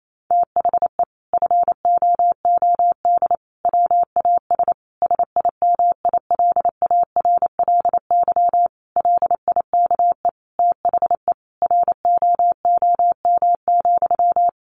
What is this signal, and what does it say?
Morse code, heard as a keyed tone, about 28 wpm, 700 Hz. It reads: THE FOOD WAS SIMILARLY LIKE THE ROOM,
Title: Sentences from Top 1400 Words 28wpm